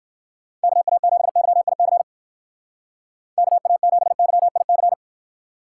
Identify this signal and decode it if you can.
Morse code, heard as a keyed tone, about 54 wpm, 700 Hz. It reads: KA7YIC KA7YIC
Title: Speed Racing - USA 2x3 Call Signs 45wpm